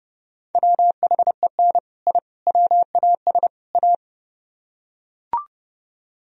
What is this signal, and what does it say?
Morse code, heard as a keyed tone, about 29 wpm, 700 Hz. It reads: WHEN I WAS A